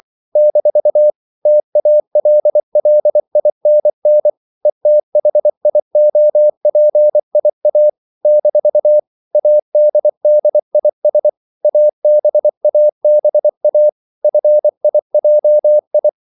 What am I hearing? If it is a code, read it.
Morse code, heard as a keyed tone, about 24 wpm, 600 Hz. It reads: - TALLINN ETHIOPIA - ADDIS ABABA FIJI
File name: CW / Practice / NationalCapitals24